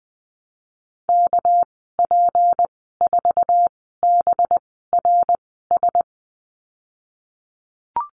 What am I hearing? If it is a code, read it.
Morse code, heard as a keyed tone, about 20 wpm, spaced as for 14 wpm, 700 Hz. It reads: KP4BRS
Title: 245 Lesson 39g - Letter z - Call Signs 20wpm